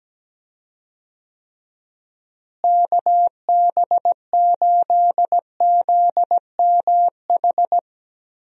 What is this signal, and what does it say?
Morse code, heard as a keyed tone, about 17 wpm, 700 Hz. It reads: KB8ZMH
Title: USA Call Signs - 6 Characters - Repeated 3x - 17wpm